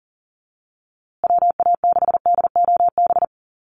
Morse code, heard as a keyed tone, about 40 wpm, 700 Hz. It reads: WA6DOB
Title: USA Call Signs Repeated 2x 40wpm